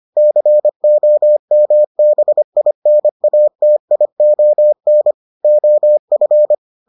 Morse code, heard as a keyed tone, about 25 wpm, 600 Hz. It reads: COMBINATION OF